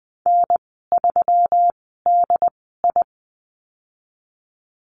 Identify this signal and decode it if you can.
Morse code, heard as a keyed tone, about 19 wpm, spaced as for 14 wpm, 700 Hz. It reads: N3DI
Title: 140 Lesson 23g - Letter f - Call Signs 20wpm